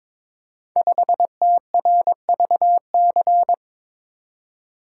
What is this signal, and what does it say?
Morse code, heard as a keyed tone, about 22 wpm, 700 Hz. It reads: 5TRVC
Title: Code Groups - 5 Letter:Numbers - 22wpm